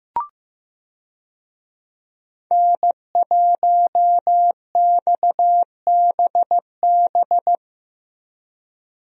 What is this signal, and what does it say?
Morse code, heard as a keyed tone, about 15 wpm, 700 Hz. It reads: N1XBB
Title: USA Call Signs - 5 Characters - 15wpm